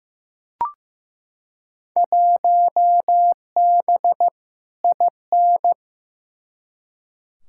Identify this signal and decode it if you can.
Morse code, heard as a keyed tone, about 15 wpm, 700 Hz. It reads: 1B IN
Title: Field Day Exchanges - 15wpm